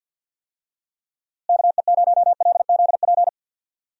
Morse code, heard as a keyed tone, about 47 wpm, 700 Hz. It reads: KE0LZP
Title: USA Call Signs - 6 Characters - Repeated 3x - 50wpm